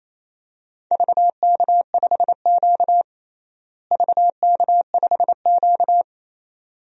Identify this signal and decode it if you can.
Morse code, heard as a keyed tone, about 28 wpm, 700 Hz. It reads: VK5Q VK5Q
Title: DXCC-100 Call Signs - 4 Characters - Repeated 2x - 28wpm